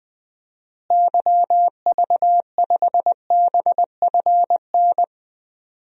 Morse code, heard as a keyed tone, about 20 wpm, 700 Hz. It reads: YV5BFN
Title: DXCC-100 Call Signs - 6 Characters - Repeated 2x - 20wpm